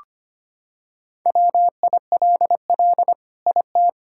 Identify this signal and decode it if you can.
Morse code, heard as a keyed tone, about 25 wpm, 700 Hz. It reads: WILL IT
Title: Sentences from Top 1600 Words 25wpm